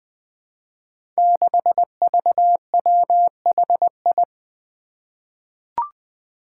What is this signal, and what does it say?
Morse code, heard as a keyed tone, about 21 wpm, 700 Hz. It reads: 6VWHI
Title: Five Challenging Letter-Number Combinations 20wpm